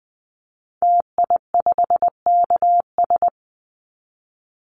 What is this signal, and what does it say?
Morse code, heard as a keyed tone, about 20 wpm, 700 Hz. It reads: TI5KS
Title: Call Signs Repeated 3x - 20wpm